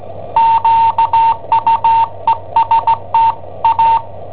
Morse code, heard as a keyed tone, about 17 wpm, 910 Hz. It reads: QUESTA
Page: Cassetta esercizi preparata da I7OHP per IW5ELP